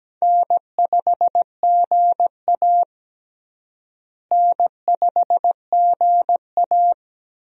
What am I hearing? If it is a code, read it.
Morse code, heard as a keyed tone, about 17 wpm, 700 Hz. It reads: N5GA N5GA
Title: USA Call Signs - 4 Characters - Repeated 3x - 17wpm